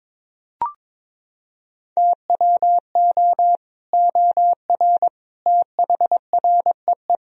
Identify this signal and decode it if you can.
Morse code, heard as a keyed tone, about 21 wpm, 700 Hz. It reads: TWO OR THREE
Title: Sentences from Top 1000 Words - Review Part 3 - 22wpm